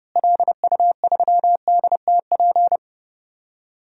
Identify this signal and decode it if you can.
Morse code, heard as a keyed tone, about 32 wpm, 700 Hz. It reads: LU3DTP
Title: DXCC-100 Call Signs - 6 Characters - Repeated 3x - 30wpm